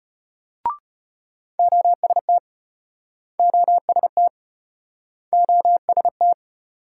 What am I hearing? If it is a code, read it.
Morse code, heard as a keyed tone, about 34 wpm, 700 Hz. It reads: OST OST OST